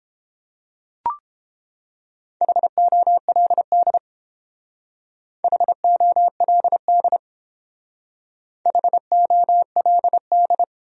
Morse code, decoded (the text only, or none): HOLD HOLD HOLD